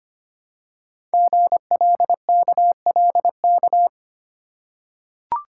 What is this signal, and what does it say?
Morse code, heard as a keyed tone, about 25 wpm, 700 Hz. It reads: GLKLK
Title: Code Groups - 5 Letters - 25wpm